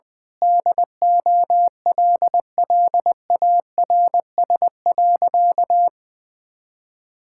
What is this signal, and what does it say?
Morse code, heard as a keyed tone, about 20 wpm, 700 Hz. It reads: DOLLARS.
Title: Sentences with 9-Word Phrases - Spoken Beginning - 20wpm